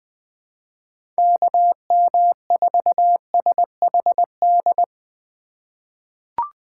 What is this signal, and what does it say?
Morse code, heard as a keyed tone, about 20 wpm, 700 Hz. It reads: KM4SHD
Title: USA Call Signs - 6 Characters - 4x Character Spacing - 20wpm